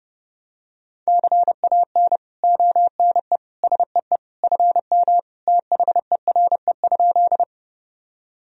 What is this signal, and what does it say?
Morse code, heard as a keyed tone, about 32 wpm, 700 Hz. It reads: CAN ONE SEE FM THERE?
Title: Sentences from Top 100 Words and Abbreviations 30wpm